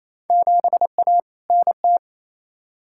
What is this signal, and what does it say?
Morse code, heard as a keyed tone, about 28 wpm, 700 Hz. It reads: 7A NT
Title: Field Day Exchanges Encore - 28wpm